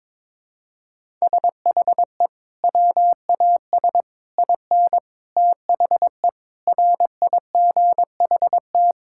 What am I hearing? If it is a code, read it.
Morse code, heard as a keyed tone, about 22 wpm, 700 Hz. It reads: SHE WAS IN THE RIGHT
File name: practice / 22wpm / Sentences with 8-Word Phrases - 22wpm